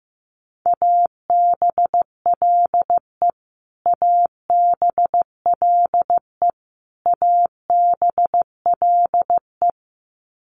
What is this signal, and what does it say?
Morse code, heard as a keyed tone, about 15 wpm, 700 Hz. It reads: ABLE ABLE ABLE
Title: Top 500 Words Repeated 15wpm